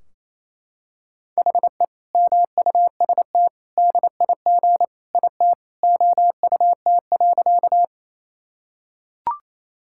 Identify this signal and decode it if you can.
Morse code, heard as a keyed tone, about 28 wpm, 700 Hz. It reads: HE MUST DIG IT OUT.
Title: Sentences from Top 1600 Words 28wpm